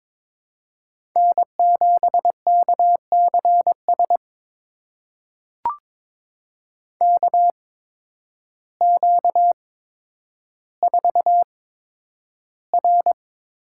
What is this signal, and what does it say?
Morse code, heard as a keyed tone, about 22 wpm, spaced as for 10 wpm, 700 Hz. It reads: N7KCS KQ4R
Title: USA Call Signs - 5 Characters - 8x Character Spacing - 22wpm